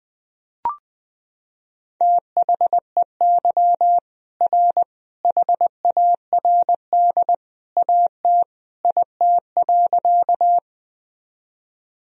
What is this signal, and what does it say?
Morse code, heard as a keyed tone, about 20 wpm, 700 Hz. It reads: THEY R HARD AT IT.
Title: Sentences from Top 300 Words and Abbreviations 20wpm